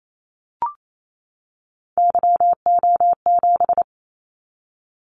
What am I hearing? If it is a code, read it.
Morse code, heard as a keyed tone, about 28 wpm, 700 Hz. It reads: YO7
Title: Callsign Prefixes for Top 25 DXCC - 28wpm